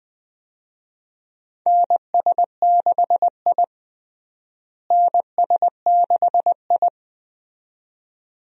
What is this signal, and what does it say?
Morse code, heard as a keyed tone, about 20 wpm, 700 Hz. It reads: NS6I NS6I
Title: USA Call Signs - 4 Characters - Repeated 2x - 20wpm